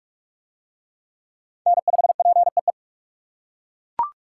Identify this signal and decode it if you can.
Morse code, heard as a keyed tone, about 41 wpm, 700 Hz. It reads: N5PEE